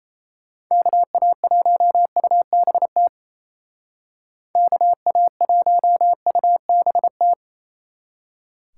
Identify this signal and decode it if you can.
Morse code, heard as a keyed tone, about 33 wpm, 700 Hz. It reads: KA1UBT KA1UBT
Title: Speed Racing - USA 2x3 Call Signs 28wpm